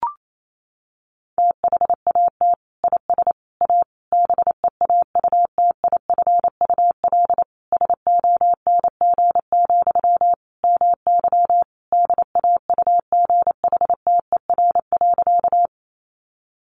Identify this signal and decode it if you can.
Morse code, heard as a keyed tone, about 28 wpm, 700 Hz. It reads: THAT IS A BEAUTIFUL SONG, MY DAUGHTER.
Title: Sentences from Top 1500 Words 28wpm